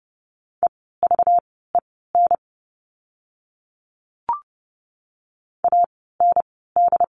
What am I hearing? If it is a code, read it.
Morse code, heard as a keyed tone, about 30 wpm, spaced as for 17 wpm, 700 Hz. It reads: EVEN AND
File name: course / 30:10wpm / 230 Lesson 37f - Letter x - Words 30wpm